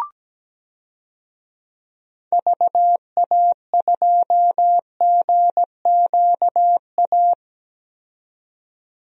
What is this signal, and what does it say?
Morse code, heard as a keyed tone, about 17 wpm, 700 Hz. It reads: VA2GQA
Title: DXCC-100 Call Signs - 6 Characters - 17wpm